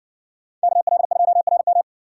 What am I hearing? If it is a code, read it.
Morse code, heard as a keyed tone, about 60 wpm, 700 Hz. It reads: KL2RM